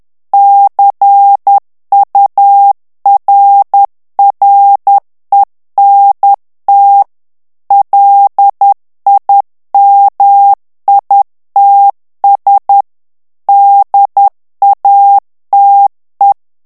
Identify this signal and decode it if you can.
Morse code, heard as a keyed tone, about 10 wpm, 800 Hz. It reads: CURRENT LIMITS DATE
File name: ANGLAIS texte clair 3 vitesse 10 mots minute